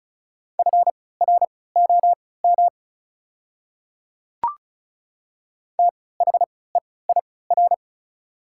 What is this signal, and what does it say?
Morse code, heard as a keyed tone, about 36 wpm, spaced as for 20 wpm, 690 Hz. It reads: FROM THEIR
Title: 3x Character Spacing - Top 100 Words - 35wpm